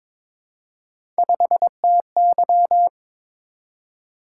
Morse code, heard as a keyed tone, about 22 wpm, 700 Hz. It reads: 5TY
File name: practice / 22wpm / Three Letter-Number Combinations - 22wpm